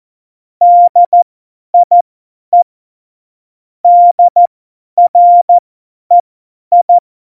Morse code, heard as a keyed tone, about 14 wpm, spaced as for 10 wpm, 700 Hz. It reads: DIE DREI
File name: Morsekurs / Witze / Witz 11 Juli 17, Tempo 50 F70